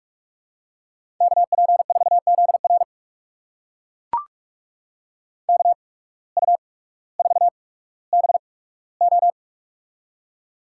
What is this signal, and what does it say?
Morse code, heard as a keyed tone, about 43 wpm, spaced as for 19 wpm, 700 Hz. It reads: KP4ZR KU4BO
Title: USA Call Signs - 5 Characters - 8x Character Spacing - 45wpm